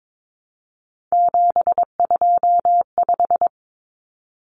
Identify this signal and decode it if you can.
Morse code, heard as a keyed tone, about 22 wpm, 700 Hz. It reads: 725
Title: Common Cut Numbers - 3-Digit 22wpm